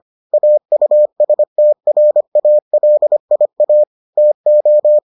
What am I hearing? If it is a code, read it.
Morse code, heard as a keyed tone, about 25 wpm, 600 Hz. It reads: AUSTRALIA TO